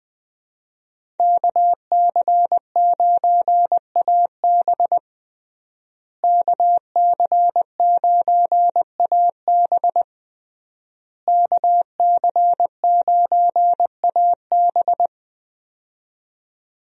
Morse code, heard as a keyed tone, about 20 wpm, 700 Hz. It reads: KC9AB KC9AB KC9AB
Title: USA Call Signs - 5 Characters - Repeated 3x - 20wpm